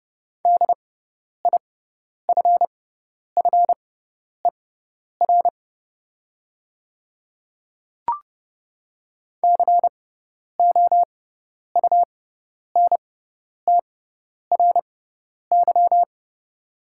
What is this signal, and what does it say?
Morse code, heard as a keyed tone, about 30 wpm, spaced as for 10 wpm, 700 Hz. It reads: DIFFER COUNTRY
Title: 6x Character Spacing - Top 500 Words - 30wpm